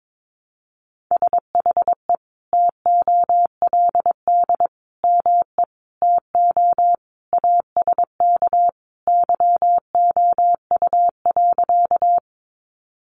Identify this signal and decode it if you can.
Morse code, heard as a keyed tone, about 22 wpm, 700 Hz. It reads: SHE TOLD ME TO ASK YOU.
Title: Sentences from Top 400 Words 22wpm